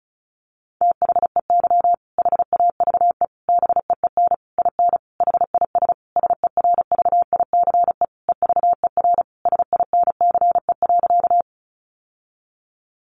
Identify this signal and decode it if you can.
Morse code, heard as a keyed tone, about 34 wpm, 700 Hz. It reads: THEY HAVE BEEN IN HIS SERVICE EVER SINCE.
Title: Sentences from Top 700 Words 35wpm